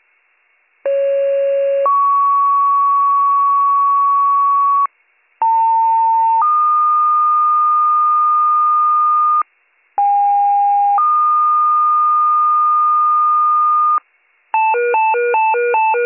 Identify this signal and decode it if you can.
Two-tone page, heard as a one-second tone, then a three-second tone, about 550 then 1120 Hz, about 900 then 1220 Hz, about 810 then 1220 Hz. It is heard from 0.8 to 14.0 s.